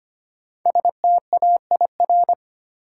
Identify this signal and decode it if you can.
Morse code, heard as a keyed tone, about 25 wpm, 700 Hz. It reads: STAIR
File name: 5 Letter Words - First 3 Letters Given - 25wpm